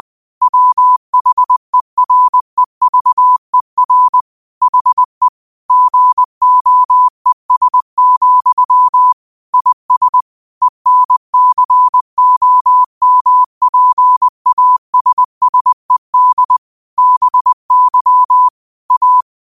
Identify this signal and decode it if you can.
Morse code, heard as a keyed tone, about 20 wpm, 1000 Hz. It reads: WHEREVER HE GOES, IS ENCOMPASSED BY A
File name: qotdfiles / 20wpm